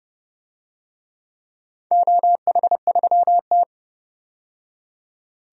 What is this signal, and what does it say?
Morse code, heard as a keyed tone, about 30 wpm, 700 Hz. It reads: OH3T
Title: DXCC-100 Call Signs - 4 Characters - 30wpm